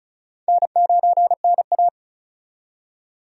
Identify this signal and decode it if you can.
Morse code, heard as a keyed tone, about 36 wpm, 700 Hz. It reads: N9NA
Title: USA Call Signs - 4 Characters - 35wpm